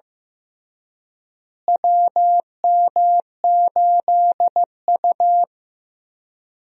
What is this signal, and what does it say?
Morse code, heard as a keyed tone, about 16 wpm, 700 Hz. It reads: WM8U
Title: USA Call Signs - 4 Characters - Repeated 3x - 15wpm